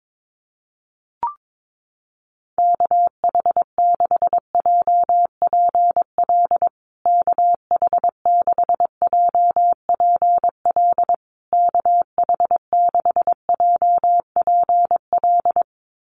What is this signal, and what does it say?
Morse code, heard as a keyed tone, about 22 wpm, 690 Hz. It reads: KH6JPL KH6JPL KH6JPL
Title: USA Call Signs Repeated 3x 22wpm